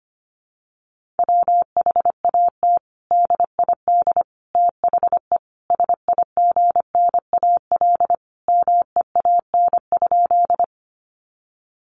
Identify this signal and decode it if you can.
Morse code, heard as a keyed tone, about 25 wpm, 700 Hz. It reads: WHAT DID THE SIGNAL MEAN?
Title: Sentences from Top 1400 Words 25wpm